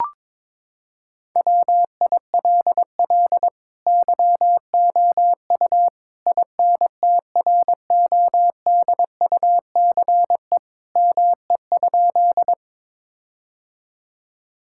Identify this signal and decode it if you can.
Morse code, heard as a keyed tone, about 22 wpm, 700 Hz. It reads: WILL YOU INTRODUCE ME?
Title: Sentences from Top 1600 Words 22wpm